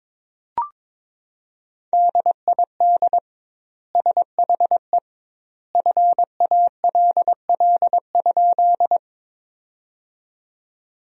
Morse code, heard as a keyed tone, about 21 wpm, 700 Hz. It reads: DID SHE FALL?
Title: 2x Word-Spacing - Sentences from Top 500 Words - 22wpm